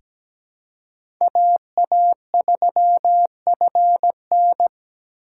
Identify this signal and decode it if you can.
Morse code, heard as a keyed tone, about 17 wpm, 700 Hz. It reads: AA3FN
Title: USA Call Signs - 5 Characters - 3x Character Spacing - 17wpm